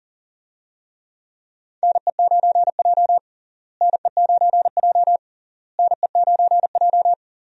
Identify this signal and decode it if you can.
Morse code, heard as a keyed tone, about 39 wpm, 700 Hz. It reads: NE9J NE9J NE9J
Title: USA Call Signs - 4 Characters - Repeated 3x - 40wpm